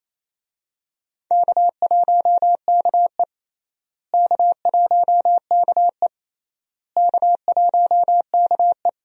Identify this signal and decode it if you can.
Morse code, heard as a keyed tone, about 28 wpm, 700 Hz. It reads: K1KE K1KE K1KE
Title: USA Call Signs - 4 Characters - Repeated 3x - 28wpm